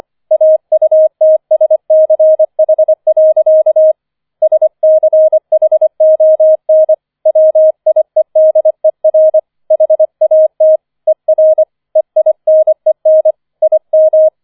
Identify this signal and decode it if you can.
Morse code, heard as a keyed tone, about 25 wpm, 630 Hz. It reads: AUTSCH. SCHON WIEDER HAT ER EINEN IM
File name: 005 Fünfter Streich 25wpm